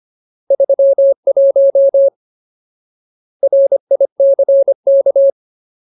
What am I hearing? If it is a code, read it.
Morse code, heard as a keyed tone, about 25 wpm, 550 Hz. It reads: 31 RICK
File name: Ultimate Word List / 25WPM Ultimate